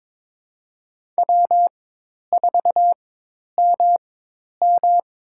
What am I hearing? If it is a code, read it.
Morse code, heard as a keyed tone, about 22 wpm, spaced as for 10 wpm, 700 Hz. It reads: W4MM
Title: USA Call Signs - 4 Characters - 4x Character Spacing - 22wpm